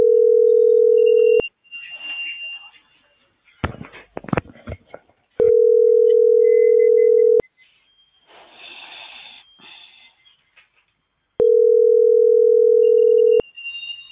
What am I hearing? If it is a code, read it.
Ringback tone (North America), 440 + 480 Hz.